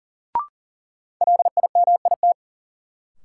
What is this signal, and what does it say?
Morse code, heard as a keyed tone, about 40 wpm, 700 Hz. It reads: LIMIT
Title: Sentences from Top 1500 Words - New Word Drill - 40wpm